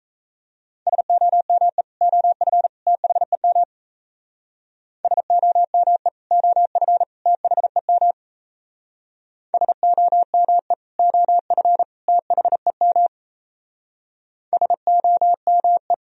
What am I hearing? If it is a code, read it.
Morse code, heard as a keyed tone, about 34 wpm, 700 Hz. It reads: SOME OF THEM SOME OF THEM SOME OF THEM SOME
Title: Speed-Racing - Sets of 3 Words 28wpm